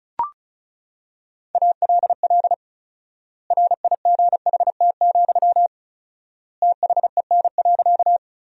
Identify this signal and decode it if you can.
Morse code, heard as a keyed tone, about 34 wpm, 700 Hz. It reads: ALL RIGHT, THEN.
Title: 4x Word-Spacing - Sentences from Top 200 Words - 35wpm